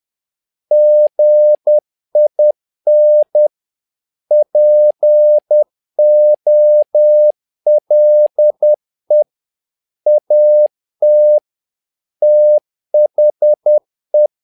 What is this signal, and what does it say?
Morse code, heard as a keyed tone, about 10 wpm, 600 Hz. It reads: GIN POLE AT THE
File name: learning center / tower trouble / tower trouble 10wpm